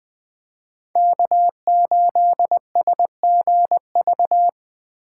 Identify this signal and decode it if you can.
Morse code, heard as a keyed tone, about 20 wpm, 700 Hz. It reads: K8SGV
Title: Speed Racing - USA 1x3 Call Signs 17wpm